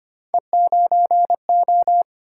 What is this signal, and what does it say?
Morse code, heard as a keyed tone, about 25 wpm, 700 Hz. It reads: E9O
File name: Three Letter-Number Combinations - 25wpm